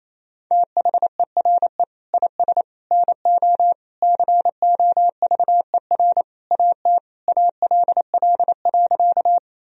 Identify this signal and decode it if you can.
Morse code, heard as a keyed tone, about 27 wpm, 700 Hz. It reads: THERE IS NO COVER AT ALL.
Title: Sentences from Top 1000 Words - Review Part 1 - 28wpm